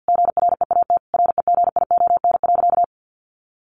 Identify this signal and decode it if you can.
Morse code, heard as a keyed tone, about 50 wpm, 700 Hz. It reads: GREAT REGION.
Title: Sentences from Top 1500 Words 50wpm